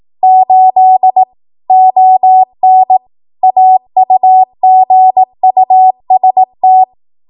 Morse code, heard as a keyed tone, about 18 wpm, 750 Hz. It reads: 8 ON AUGUST
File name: Archive / 18 WPM / 18WPM